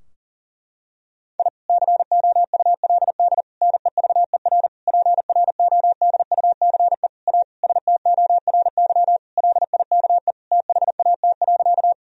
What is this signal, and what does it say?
Morse code, heard as a keyed tone, about 40 wpm, 700 Hz. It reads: I COULD NEVER PRODUCE A STORY LIKE THAT.
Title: Sentences from Top 1000 Words - Review Part 3 - 40wpm